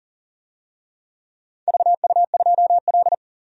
Morse code, heard as a keyed tone, about 39 wpm, 700 Hz. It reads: VU2L
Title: DXCC-100 Call Signs - 4 Characters - 40wpm